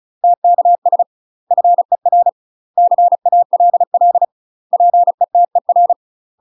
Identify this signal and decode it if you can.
Morse code, heard as a keyed tone, about 34 wpm, 700 Hz. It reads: TKS FER CALL PETER
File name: UK MASTER QSOs TRAINING / 35WPM / UK MASTER QSOs 0003